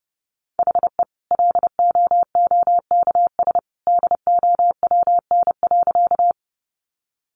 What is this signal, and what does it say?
Morse code, heard as a keyed tone, about 28 wpm, 700 Hz. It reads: HE LOOKS DOWN.
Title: Additional Easy Phrases 30wpm